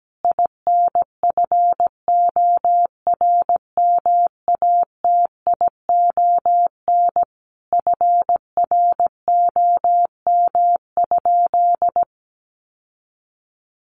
Morse code, heard as a keyed tone, about 17 wpm, 700 Hz. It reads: INFORMATION FROM?
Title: Sentences from Top 600 Words 17wpm